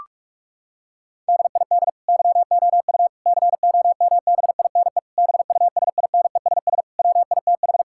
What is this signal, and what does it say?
Morse code, heard as a keyed tone, about 46 wpm, 700 Hz. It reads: DID YOU COMBINE BUSINESS WITH
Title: Sentences from Top 1600 Words 45wpm